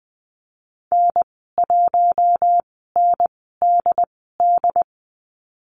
Morse code, heard as a keyed tone, about 20 wpm, spaced as for 14 wpm, 700 Hz. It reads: N1NDD
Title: 069 Lesson 12g - Letter d - Call Signs 20wpm